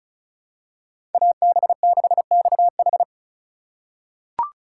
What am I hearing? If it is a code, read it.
Morse code, heard as a keyed tone, about 36 wpm, 690 Hz. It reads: AB6XH
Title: USA Call Signs - 5 Characters - 3x Character Spacing - 35wpm